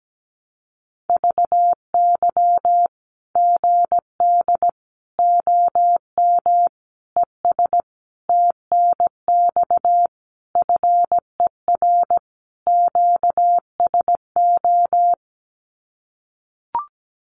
Morse code, heard as a keyed tone, about 17 wpm, 700 Hz. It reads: VY GD OM ES TNX FER QSO
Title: Common QSO Phrases 17wpm